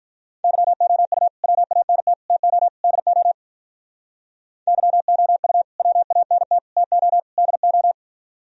Morse code, heard as a keyed tone, about 46 wpm, 700 Hz. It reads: YOU WANT TO DO YOU WANT TO DO